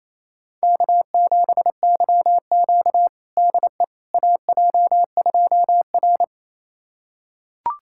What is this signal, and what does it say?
Morse code, heard as a keyed tone, about 27 wpm, 700 Hz. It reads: K7YQ DE AJ2R
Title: USA Callsign DE Callsign - 4 Characters - 28wpm